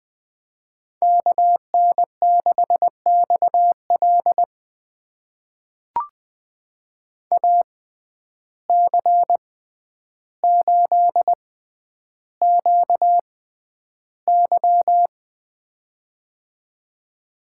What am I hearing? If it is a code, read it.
Morse code, heard as a keyed tone, about 20 wpm, spaced as for 10 wpm, 700 Hz. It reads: KN6XL AC8QY
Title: USA Call Signs - 5 Characters - 6x Character Spacing - 20wpm